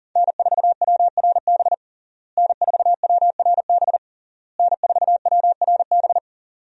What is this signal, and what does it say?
Morse code, heard as a keyed tone, about 40 wpm, 700 Hz. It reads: N4WRB N4WRB N4WRB
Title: USA Call Signs - 5 Characters - Repeated 3x - 40wpm